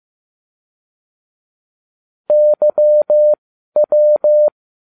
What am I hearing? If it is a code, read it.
Morse code, heard as a keyed tone, about 15 wpm, spaced as for 12 wpm, 600 Hz. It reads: YW
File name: cw / ubacw20 3